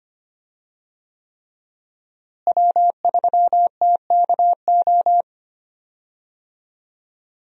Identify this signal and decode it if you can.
Morse code, heard as a keyed tone, about 25 wpm, 700 Hz. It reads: W3TKO
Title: USA Call Signs - 5 Characters - Repeated 3x - 25wpm